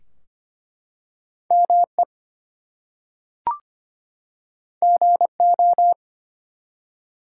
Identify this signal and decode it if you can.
Morse code, heard as a keyed tone, about 25 wpm, 700 Hz. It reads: ME GO
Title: Two Letter Words - 25wpm